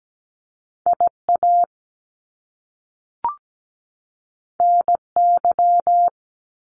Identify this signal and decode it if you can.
Morse code, heard as a keyed tone, about 17 wpm, 700 Hz. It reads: IA NY
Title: US States 17wpm